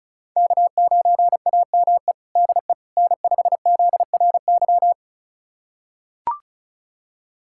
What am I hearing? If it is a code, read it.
Morse code, heard as a keyed tone, about 36 wpm, 700 Hz. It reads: K9AME DE N5ZRY
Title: USA Callsign DE Callsign - 5 Characters - 35wpm